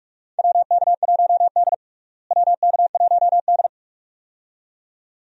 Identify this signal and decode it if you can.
Morse code, heard as a keyed tone, about 42 wpm, 700 Hz. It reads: WK1D WK1D
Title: USA Call Signs - 4 Characters - Repeated 2x - 45wpm